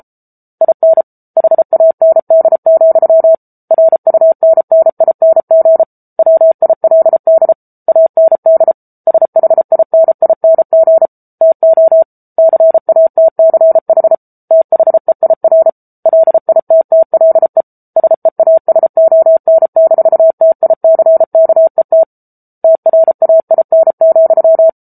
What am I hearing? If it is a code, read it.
Morse code, heard as a keyed tone, about 34 wpm, 650 Hz. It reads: IN HAND, RUNNING WILD AND SHINING TO CATCH THEIR LITTLE SEASON-TICKET TRAIN,
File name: Book Two / 07-Chapter 7 - 33 WPM